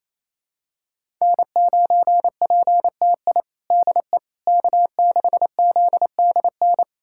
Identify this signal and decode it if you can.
Morse code, heard as a keyed tone, about 28 wpm, 700 Hz. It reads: N9PTI DE K6ZDN
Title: USA Callsign DE Callsign - 5 Characters - 28wpm